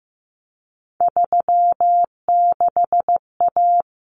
Morse code, heard as a keyed tone, about 15 wpm, 700 Hz. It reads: 36A
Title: Common Cut Numbers - 3-Digit 15wpm